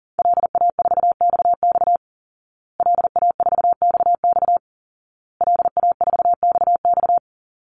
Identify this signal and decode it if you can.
Morse code, heard as a keyed tone, about 40 wpm, 700 Hz. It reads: LA4XX LA4XX LA4XX
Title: Call Signs Repeated 2x - 40wpm